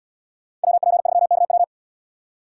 Morse code, heard as a keyed tone, about 71 wpm, 700 Hz. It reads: WP3KW